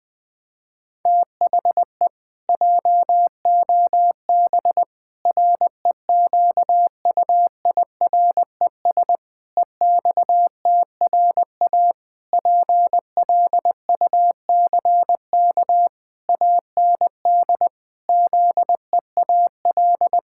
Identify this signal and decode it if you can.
Morse code, heard as a keyed tone, about 20 wpm, 700 Hz. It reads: THE JOB REQUIRES EXTRA PLUCK AND ZEAL
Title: Pangram Challenge v2 - 20wpm